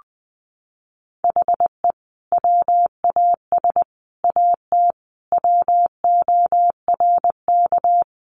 Morse code, heard as a keyed tone, about 20 wpm, 700 Hz. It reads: HE WAS AT WORK
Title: Sentences from Top 200 Words 20wpm